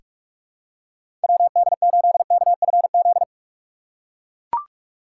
Morse code, heard as a keyed tone, about 44 wpm, 700 Hz. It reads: WD8KFZ
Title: USA Call Signs - 6 Characters - 3x Character Spacing - 45wpm